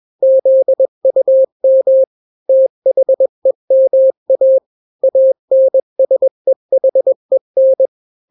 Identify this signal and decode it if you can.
Morse code, heard as a keyed tone, about 21 wpm, 530 Hz. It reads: ZUM THEMA ANSEHEN
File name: daily / 20wpm